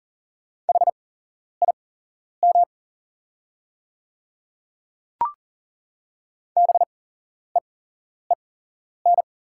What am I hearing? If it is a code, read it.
Morse code, heard as a keyed tone, about 40 wpm, spaced as for 11 wpm, 690 Hz. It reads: HIM BEEN
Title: 8x Character Spacing - Top 300 Words - 40wpm